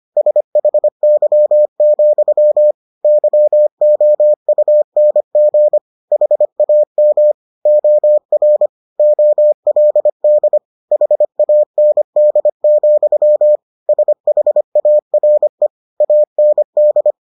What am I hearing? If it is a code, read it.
Morse code, heard as a keyed tone, about 25 wpm, 600 Hz. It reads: SHY, YOUNG HAM OR OLD HAND, SHARE AND